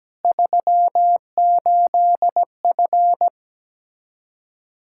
Morse code, heard as a keyed tone, about 17 wpm, 700 Hz. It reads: 38F
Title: Practice QSO - Exchange 2 - Info - 17wpm